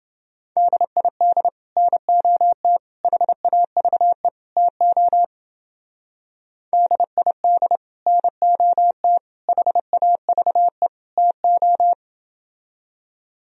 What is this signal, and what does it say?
Morse code, heard as a keyed tone, about 28 wpm, 700 Hz. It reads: DID NOT HAVE TO DID NOT HAVE TO